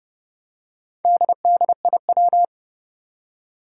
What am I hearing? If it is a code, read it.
Morse code, heard as a keyed tone, about 30 wpm, 700 Hz. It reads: DDIW
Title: Four Challenging Letter-Number Combinations 30wpm